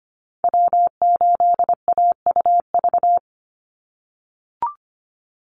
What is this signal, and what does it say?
Morse code, heard as a keyed tone, about 25 wpm, 700 Hz. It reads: W8AUV
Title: USA Call Signs 25wpm